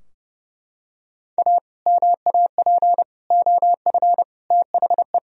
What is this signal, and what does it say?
Morse code, heard as a keyed tone, about 30 wpm, 700 Hz. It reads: A MAP OF THE
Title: Sentences from Top 1000 Words - Review Part 2 - 30wpm